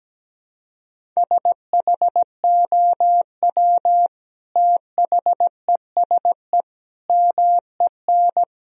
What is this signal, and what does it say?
Morse code, heard as a keyed tone, about 17 wpm, 700 Hz. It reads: SHOW THESE MEN
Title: Sentences from Top 1000 Words - Review Part 1 - 17wpm